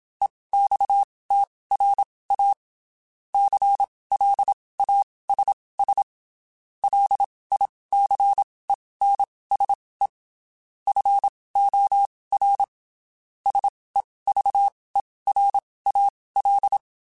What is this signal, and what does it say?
Morse code, heard as a keyed tone, about 27 wpm, spaced as for 19 wpm, 800 Hz. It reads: EXTRA CLASS LICENSE FOR SEVERAL
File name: (spaced) / QSOs Spaced-mp3s / WA7QOX 2 @25 sp